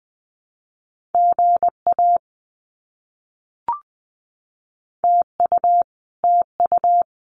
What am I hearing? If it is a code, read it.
Morse code, heard as a keyed tone, about 20 wpm, 700 Hz. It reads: GA TU TU
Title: Abbreviations Repeated 3x - 20wpm